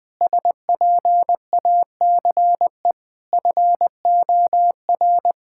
Morse code, heard as a keyed tone, about 20 wpm, 700 Hz. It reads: SPACE FOR
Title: Sentences from Top 1000 Words - Review Part 5 - 20wpm